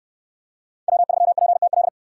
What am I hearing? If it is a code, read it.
Morse code, heard as a keyed tone, about 68 wpm, 700 Hz. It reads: W3PTL